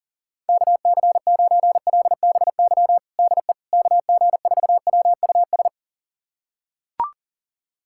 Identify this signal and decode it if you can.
Morse code, heard as a keyed tone, about 40 wpm, 700 Hz. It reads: KC9LBY DE KG4WUS
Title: USA Callsign DE Callsign - 6 Characters - 40wpm